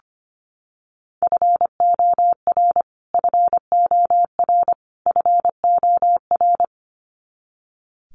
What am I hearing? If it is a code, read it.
Morse code, heard as a keyed tone, about 25 wpm, 700 Hz. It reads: FOR FOR FOR
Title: Top 500 Words Repeated 25wpm